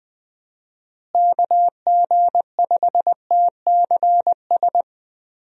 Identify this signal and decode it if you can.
Morse code, heard as a keyed tone, about 20 wpm, 700 Hz. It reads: KG5TCS
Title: USA Call Signs - 6 Characters - Repeated 2x - 20wpm